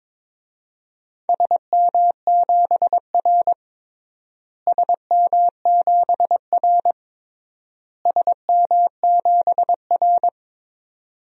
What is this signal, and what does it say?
Morse code, heard as a keyed tone, about 22 wpm, 700 Hz. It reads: SM7R SM7R SM7R
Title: DXCC-100 Call Signs - 4 Characters - Repeated 3x - 22wpm